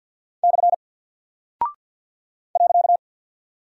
Morse code, heard as a keyed tone, about 49 wpm, 700 Hz. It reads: / .
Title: Single Character (Letter:Number:Punctuation:Prosign) 50wpm